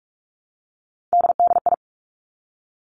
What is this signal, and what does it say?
Morse code, heard as a keyed tone, about 46 wpm, 700 Hz. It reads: DDI
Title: Three Challenging Letter-Number Combinations 45wpm